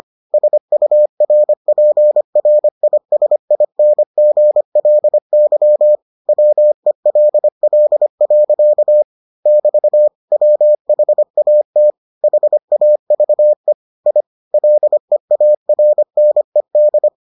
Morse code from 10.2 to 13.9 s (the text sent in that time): WHAT HAVE